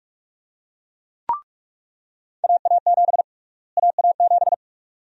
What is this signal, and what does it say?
Morse code, heard as a keyed tone, about 46 wpm, 700 Hz. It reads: AA7 AA7
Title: USA Call Sign Prefixes - Repeated 2x - 45wpm